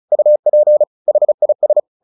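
Morse code, heard as a keyed tone, about 35 wpm, 600 Hz. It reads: UP HIS
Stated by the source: Morse practice at 35 wpm from CW med Gnister, sending Ken Follett's novel Never in English.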